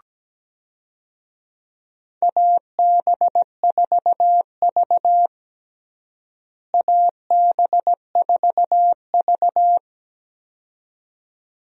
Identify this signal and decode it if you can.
Morse code, heard as a keyed tone, about 17 wpm, 700 Hz. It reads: AB4V AB4V
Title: USA Call Signs - 4 Characters - Repeated 2x - 17wpm